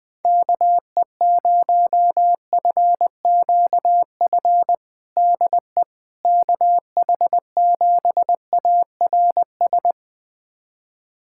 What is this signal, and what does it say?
Morse code, heard as a keyed tone, about 20 wpm, 700 Hz. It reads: KE0FQF DE KH7ARS
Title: USA Callsign DE Callsign - 6 Characters - 20wpm